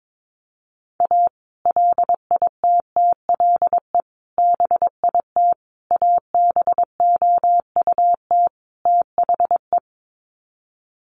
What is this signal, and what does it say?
Morse code, heard as a keyed tone, about 22 wpm, 700 Hz. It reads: A LITTLE BIT ABOUT THE
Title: Sets of 5 Words Encore2 22wpm